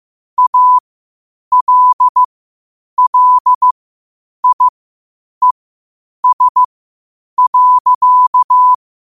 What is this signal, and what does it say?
Morse code, heard as a keyed tone, about 15 wpm, spaced as for 8 wpm, 1000 Hz. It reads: ALLIES.
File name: qotdfiles / 8wpm